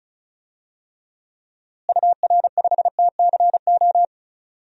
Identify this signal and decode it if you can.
Morse code, heard as a keyed tone, about 36 wpm, 700 Hz. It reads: UR5TCO
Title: DXCC-100 Call Signs - 6 Characters - Repeated 2x - 35wpm